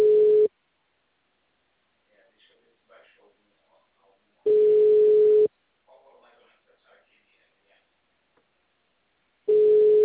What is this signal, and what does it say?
Ringback tone (Europe, 425 Hz), with speech.